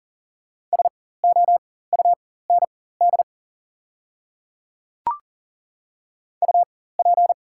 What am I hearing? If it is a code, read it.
Morse code, heard as a keyed tone, about 40 wpm, spaced as for 19 wpm, 690 Hz. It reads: SOUND UP
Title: 4x Character Spacing - Top 100 Words - 40wpm